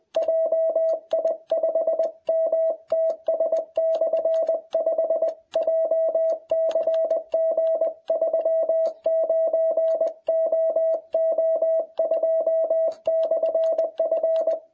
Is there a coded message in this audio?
Yes: Morse code from 0.2 to 14.5 s, heard as a keyed tone, about 21 wpm, 660 Hz. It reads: JI5MTS/5 JCG39OO2/F